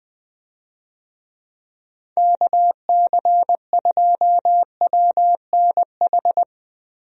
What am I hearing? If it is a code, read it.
Morse code, heard as a keyed tone, about 20 wpm, 700 Hz. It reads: KC2WNH
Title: USA Call Signs - 6 Characters - Repeated 3x - 20wpm